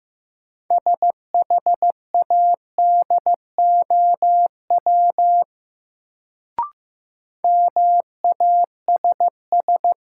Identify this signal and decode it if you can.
Morse code, heard as a keyed tone, about 15 wpm, 700 Hz. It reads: SHADOW MASS
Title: Sets of 5 Words Encore - New Word Drill - 15wpm